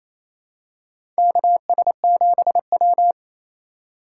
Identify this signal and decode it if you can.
Morse code, heard as a keyed tone, about 28 wpm, 700 Hz. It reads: KS7W
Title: USA Call Signs - 4 Characters - 8x Character Spacing - 28wpm